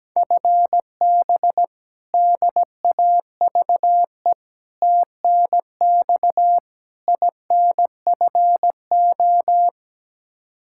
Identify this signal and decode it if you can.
Morse code, heard as a keyed tone, about 17 wpm, 700 Hz. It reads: FB DAVE TNX INFO
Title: K1USN SST QSO Practice 17wpm